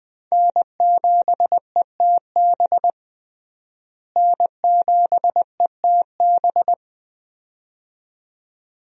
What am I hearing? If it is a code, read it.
Morse code, heard as a keyed tone, about 20 wpm, 700 Hz. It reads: N7ETB N7ETB
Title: USA Call Signs - 5 Characters - Repeated 2x - 20wpm